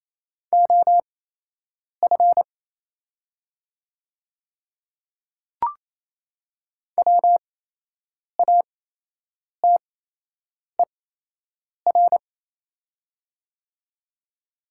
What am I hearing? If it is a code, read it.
Morse code, heard as a keyed tone, about 28 wpm, spaced as for 8 wpm, 700 Hz. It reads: OF WATER
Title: 8x Character Spacing - Top 100 Words - 28wpm